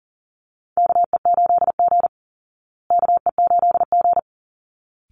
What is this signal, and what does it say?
Morse code, heard as a keyed tone, about 40 wpm, 700 Hz. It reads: KE8G KE8G
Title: Call Signs Repeated 3x - 40wpm